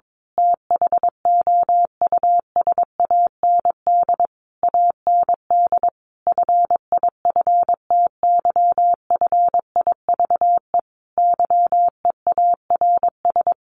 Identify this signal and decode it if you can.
Morse code, heard as a keyed tone, about 22 wpm, 700 Hz. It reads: THOUSAND AND FIFTY FIVE YEARS
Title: A Sound Of Thunder 22wpm